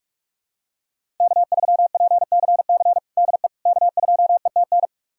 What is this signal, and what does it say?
Morse code, heard as a keyed tone, about 46 wpm, 700 Hz. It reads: K3PCC DE K2ETN
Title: USA Callsign DE Callsign - 5 Characters - 45wpm